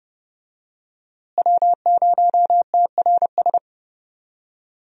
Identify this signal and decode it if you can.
Morse code, heard as a keyed tone, about 31 wpm, 700 Hz. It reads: W0TRS
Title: USA Call Signs - 5 Characters - Repeated 2x - 30wpm